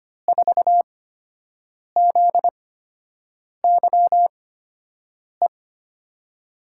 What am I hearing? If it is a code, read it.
Morse code, heard as a keyed tone, about 25 wpm, spaced as for 7 wpm, 700 Hz. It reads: 4ZYE
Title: USA Call Signs - 6 Characters - 8x Character Spacing - 25wpm